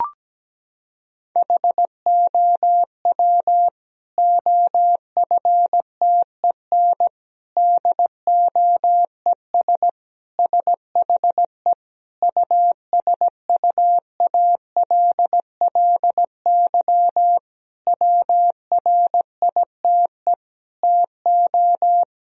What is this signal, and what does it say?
Morse code, heard as a keyed tone, about 17 wpm, 700 Hz. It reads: HOW OFTEN DOES SHE USUALLY WRITE TO
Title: Sentences from Top 1000 Words - Review Part 3 - 17wpm